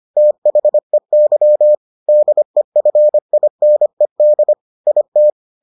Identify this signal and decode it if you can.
Morse code, heard as a keyed tone, about 25 wpm, 600 Hz. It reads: THEY DEFINED IT